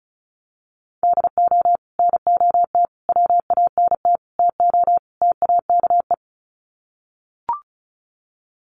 Morse code, heard as a keyed tone, about 34 wpm, 700 Hz. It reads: DO NOT WANT TO TAKE
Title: Sets of 5 Words Encore2 35wpm